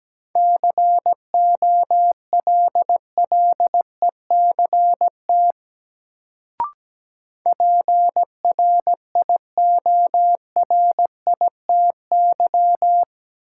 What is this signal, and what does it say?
Morse code, heard as a keyed tone, about 17 wpm, 700 Hz. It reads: COLLECT PRIORITY
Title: Sentences from Top 1600 Words - New Word Drill Alt - 17wpm